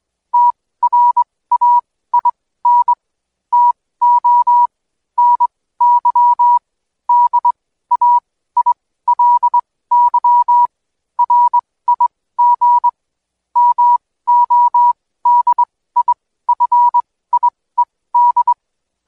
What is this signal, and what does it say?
Morse code, heard as a keyed tone, about 22 wpm, spaced as for 16 wpm, 970 Hz. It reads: TRAIN TO NY DAILY RIG MODIFIED
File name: Code Quick / AudioDownld2 / Seg125-DayDrill30-16WPM